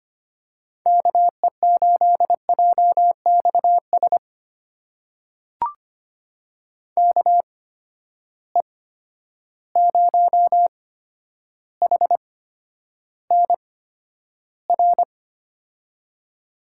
Morse code, heard as a keyed tone, about 25 wpm, spaced as for 11 wpm, 700 Hz. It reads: KE8JXS KE0HNR
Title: USA Call Signs - 6 Characters - 8x Character Spacing - 25wpm